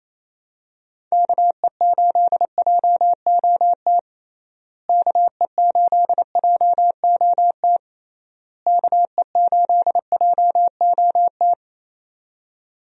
Morse code, heard as a keyed tone, about 28 wpm, 700 Hz. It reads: KE8JOT KE8JOT KE8JOT
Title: USA Call Signs - 6 Characters - Repeated 3x - 28wpm